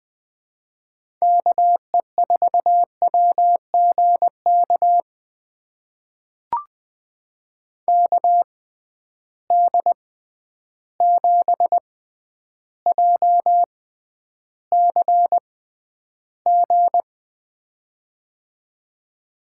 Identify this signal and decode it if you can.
Morse code, heard as a keyed tone, about 20 wpm, spaced as for 10 wpm, 700 Hz. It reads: KE4WGK KD7JCG
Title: USA Call Signs - 6 Characters - 6x Character Spacing - 20wpm